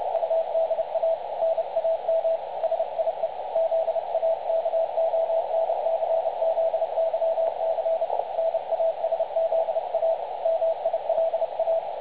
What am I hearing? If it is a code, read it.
Morse code, heard as a keyed tone, about 31 wpm, 660 Hz, with radio noise. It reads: UDA NAM SIE ZAKONCZYC NASZA MILA
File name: dlazaawansowanych / nagraniazpasma / websdr recording 3565.0kHz